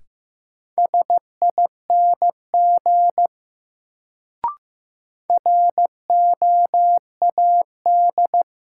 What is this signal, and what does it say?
Morse code, heard as a keyed tone, about 15 wpm, 700 Hz. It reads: SING ROAD
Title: Sentences from Top 400 Words - New Word Drill - 15wpm